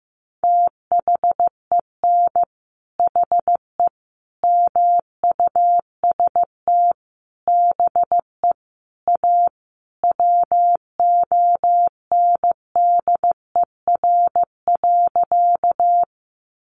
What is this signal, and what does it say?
Morse code, heard as a keyed tone, about 15 wpm, 700 Hz. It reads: THEN HE MUST BE A WONDER.
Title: Sentences from Top 500 Words 15wpm